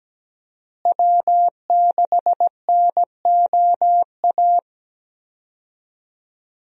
Morse code, heard as a keyed tone, about 17 wpm, 700 Hz. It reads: W6NOA
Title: USA Call Signs - 5 Characters - Repeated 2x - 17wpm